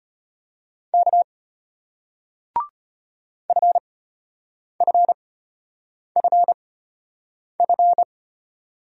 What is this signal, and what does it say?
Morse code, heard as a keyed tone, about 33 wpm, spaced as for 8 wpm, 700 Hz. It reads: K FFFF